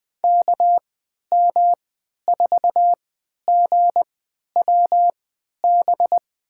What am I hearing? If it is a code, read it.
Morse code, heard as a keyed tone, about 21 wpm, spaced as for 12 wpm, 700 Hz. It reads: KM4GWB
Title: USA Call Signs - 6 Characters - 3x Character Spacing - 20wpm